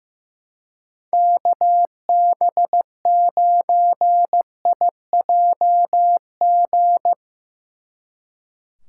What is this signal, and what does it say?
Morse code, heard as a keyed tone, about 16 wpm, 700 Hz. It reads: KB9IJG
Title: Speed Racing - USA 2x3 Call Signs 15wpm